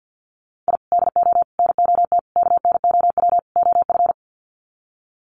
Morse code, heard as a keyed tone, about 50 wpm, 700 Hz. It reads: I DO NOT KNOW OF
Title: Sets of 5 Words Encore2 50wpm